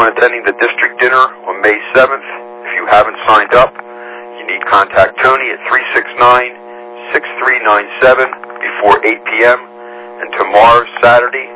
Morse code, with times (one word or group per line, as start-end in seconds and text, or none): none